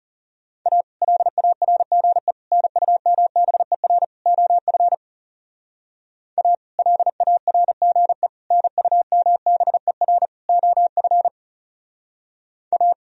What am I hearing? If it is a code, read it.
Morse code, heard as a keyed tone, about 34 wpm, 690 Hz. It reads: A LARGE NUMBER OF A LARGE NUMBER OF A
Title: Speed-Racing - Sets of 4 Words 30wpm